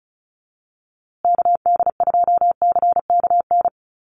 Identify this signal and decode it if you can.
Morse code, heard as a keyed tone, about 34 wpm, 700 Hz. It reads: KD2CKN
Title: USA Call Signs Repeated 2x 35wpm